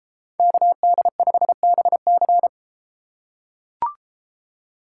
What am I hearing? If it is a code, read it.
Morse code, heard as a keyed tone, about 32 wpm, 700 Hz. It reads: KD5BC